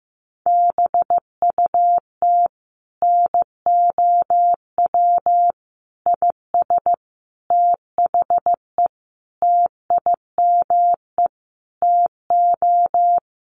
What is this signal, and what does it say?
Morse code, heard as a keyed tone, about 15 wpm, 700 Hz. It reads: BUT NOW IS THE TIME TO
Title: Sentences from Top 1500 Words 15wpm